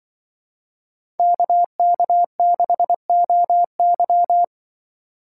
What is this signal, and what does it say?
Morse code, heard as a keyed tone, about 23 wpm, 700 Hz. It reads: KK6OY
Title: Speed Racing - USA 2x2 Call Signs 20wpm